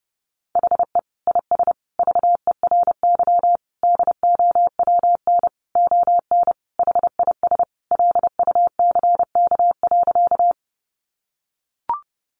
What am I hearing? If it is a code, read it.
Morse code, heard as a keyed tone, about 30 wpm, 700 Hz. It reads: HE IS VERY DOWN ON HIS LUCK.
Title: Sentences from Top 1500 Words 30wpm